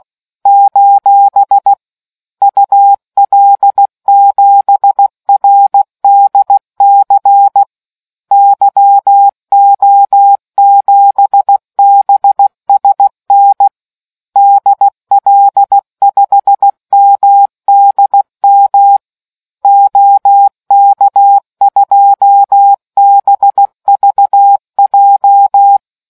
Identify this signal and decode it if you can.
Morse code, heard as a keyed tone, about 16 wpm, 800 Hz. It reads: : UL7RDC YO7BSN DL5MDM OK2BVJ